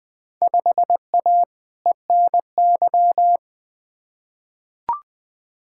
Morse code, heard as a keyed tone, about 20 wpm, 700 Hz. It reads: 5A ENY
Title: Field Day Exchanges Encore - 20wpm